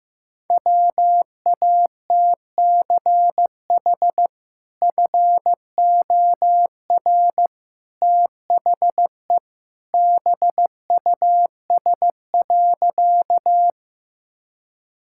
Morse code, heard as a keyed tone, about 16 wpm, 700 Hz. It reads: WATCH FOR THE BUS.
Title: Sentences from Top 1000 Words - Review Part 6 - 15wpm